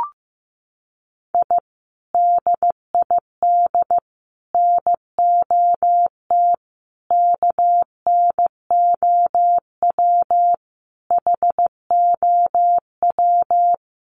Morse code, heard as a keyed tone, about 15 wpm, 700 Hz. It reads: I DID NOT KNOW HOW
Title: Sets of 5 Words Encore 15wpm